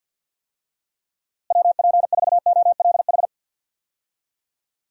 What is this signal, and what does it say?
Morse code, heard as a keyed tone, about 52 wpm, 700 Hz. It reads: WP4OLH
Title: USA Call Signs - 6 Characters - Repeated 3x - 50wpm